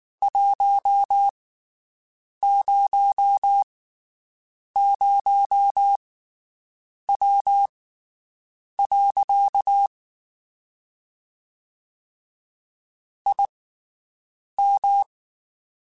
Morse code, heard as a keyed tone, about 19 wpm, spaced as for 7 wpm, 800 Hz. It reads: 100W. IM
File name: QSOs Spaced